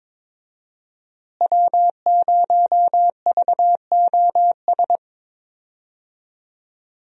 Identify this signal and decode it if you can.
Morse code, heard as a keyed tone, about 22 wpm, 700 Hz. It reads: W0VOS
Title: USA Call Signs - 5 Characters - Repeated 3x - 22wpm